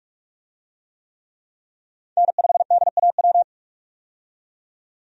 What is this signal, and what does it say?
Morse code, heard as a keyed tone, about 46 wpm, 700 Hz. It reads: N5DAW
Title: USA Call Signs - 5 Characters - 45wpm